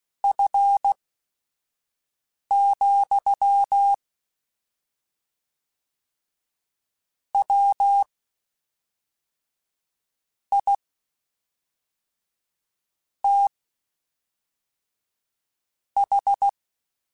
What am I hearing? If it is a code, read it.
Morse code, heard as a keyed tone, about 16 wpm, spaced as for 4 wpm, 800 Hz. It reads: F, WITH
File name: Spaced Characters / Spaced Characters-mp3s / Weather5 5 sp @5